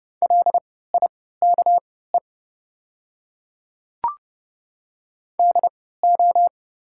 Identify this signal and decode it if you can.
Morse code, heard as a keyed tone, about 30 wpm, spaced as for 17 wpm, 700 Hz. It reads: LIKE DO
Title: 3x Character Spacing - Top 100 Words - 30wpm